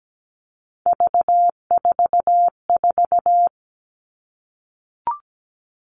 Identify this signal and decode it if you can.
Morse code, heard as a keyed tone, about 17 wpm, 700 Hz. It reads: V44
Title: All Cut Numbers - 3-Digit 17wpm